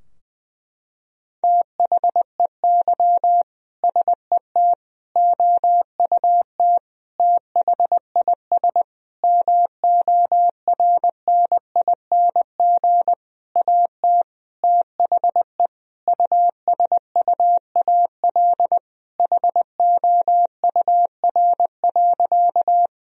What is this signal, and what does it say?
Morse code, heard as a keyed tone, about 20 wpm, 700 Hz. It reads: THEY SET OUT THIS MORNING AT THE USUAL HOUR.
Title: Sentences from Top 1000 Words - Review Part 2 - 20wpm